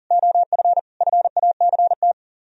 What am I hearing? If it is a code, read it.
Morse code, heard as a keyed tone, about 50 wpm, 700 Hz. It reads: OF FACT
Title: Speed-Racing - Sets of 5 Words 40wpm